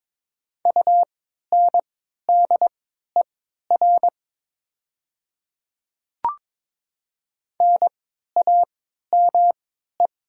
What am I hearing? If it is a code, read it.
Morse code, heard as a keyed tone, about 22 wpm, spaced as for 12 wpm, 700 Hz. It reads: UNDER NAME
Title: 3x Character Spacing - Top 300 Words - 22wpm